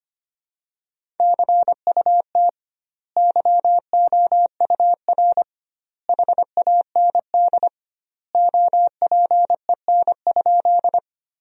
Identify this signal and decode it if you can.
Morse code, heard as a keyed tone, about 25 wpm, 700 Hz. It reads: CUT YOUR HAND OPEN?
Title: 2x Word-Spacing - Sentences from Top 300 Words - 25wpm